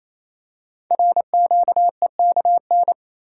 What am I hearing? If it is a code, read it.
Morse code, heard as a keyed tone, about 27 wpm, 700 Hz. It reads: RQEKN
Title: Code Groups - 5 Letters - 28wpm